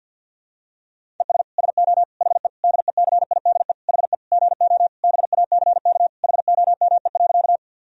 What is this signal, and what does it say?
Morse code, heard as a keyed tone, about 52 wpm, 700 Hz. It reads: ES SO HE DECIDE HE GO BACK HOME.
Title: Sentences from Top 500 Words and Abbreviations 50wpm